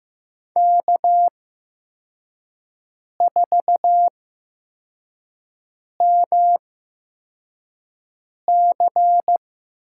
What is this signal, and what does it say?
Morse code, heard as a keyed tone, about 15 wpm, spaced as for 4 wpm, 700 Hz. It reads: K4MC
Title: USA Call Signs - 5 Characters - 8x Character Spacing - 15wpm